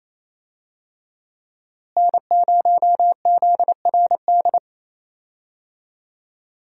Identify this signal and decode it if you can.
Morse code, heard as a keyed tone, about 28 wpm, 700 Hz. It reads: N0ZRD
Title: USA Call Signs - 5 Characters - Repeated 2x - 28wpm